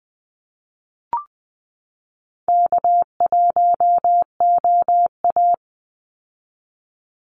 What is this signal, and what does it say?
Morse code, heard as a keyed tone, about 20 wpm, 700 Hz. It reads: K1OA
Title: Call Signs Repeated 2x - 20wpm